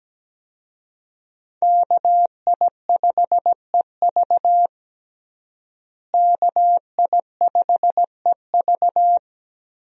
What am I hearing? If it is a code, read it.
Morse code, heard as a keyed tone, about 17 wpm, 700 Hz. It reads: KI5EV KI5EV
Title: USA Call Signs - 5 Characters - Repeated 3x - 17wpm